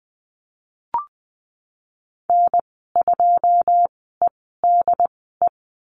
Morse code, heard as a keyed tone, about 20 wpm, spaced as for 14 wpm, 700 Hz. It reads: N2EDE
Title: 088 Lesson 15g - Number 5 - Call Signs 20wpm